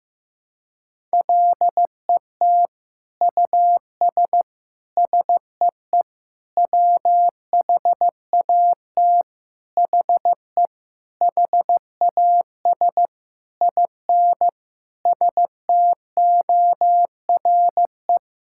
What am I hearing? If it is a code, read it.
Morse code, heard as a keyed tone, about 15 wpm, 700 Hz. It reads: LET US SEE WHAT HE HAS IN STORE
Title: Sentences from Top 1000 Words - Review Part 5 - 15wpm